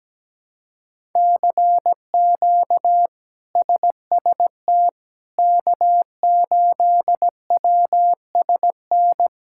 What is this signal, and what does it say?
Morse code, heard as a keyed tone, about 17 wpm, 700 Hz. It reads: CQ SST K8WSN
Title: K1USN SST QSO Practice - 17wpm